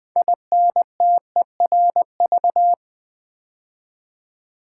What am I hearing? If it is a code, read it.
Morse code, heard as a keyed tone, about 20 wpm, 700 Hz. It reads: INTERV
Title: Build A Word - 3 Letters At A Time - 9-Letter Words - 20wpm